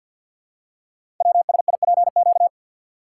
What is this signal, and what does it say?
Morse code, heard as a keyed tone, about 48 wpm, 700 Hz. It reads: WSIPQ
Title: Code Groups - 5 Letters - 50wpm